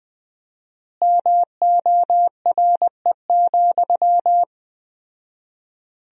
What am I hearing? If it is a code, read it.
Morse code, heard as a keyed tone, about 20 wpm, 700 Hz. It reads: MORE,
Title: 4x Word-Spacing - Sentences from Top 300 Words - 20wpm